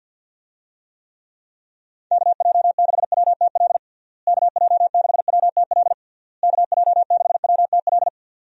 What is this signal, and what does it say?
Morse code, heard as a keyed tone, about 52 wpm, 700 Hz. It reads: KJ6WTL KJ6WTL KJ6WTL
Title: USA Call Signs - 6 Characters - Repeated 3x - 50wpm